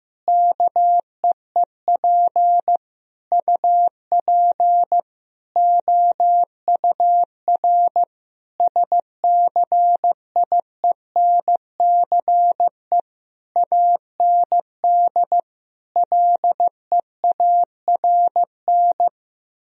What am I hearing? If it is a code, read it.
Morse code, heard as a keyed tone, about 15 wpm, 700 Hz. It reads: KEEP UP OUR SCIENCE AND LEARN
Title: Sentences from Top 1000 Words - Review Part 4 - 15wpm